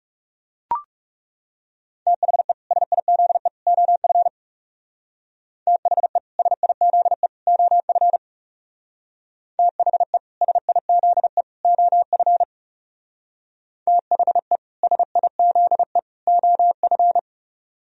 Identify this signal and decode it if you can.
Morse code, heard as a keyed tone, about 39 wpm, 700 Hz. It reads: THE SIZE OF THE SIZE OF THE SIZE OF THE SIZE OF